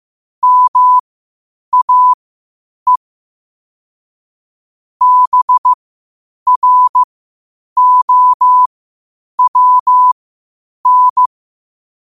Morse code, heard as a keyed tone, about 15 wpm, spaced as for 8 wpm, 1000 Hz. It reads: MAE BROWN
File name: qotdfiles / 8wpm